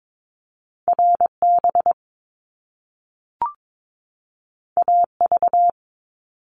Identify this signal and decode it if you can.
Morse code, heard as a keyed tone, about 22 wpm, 700 Hz. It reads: RB AV
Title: Top Two Letter Combinations in Words 22wpm